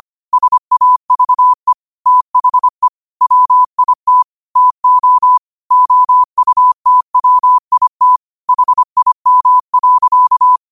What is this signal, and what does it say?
Morse code, heard as a keyed tone, about 25 wpm, 1000 Hz. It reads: SAVE THE WIT TO OUTWIT HIM.
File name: qotdfiles / 25wpm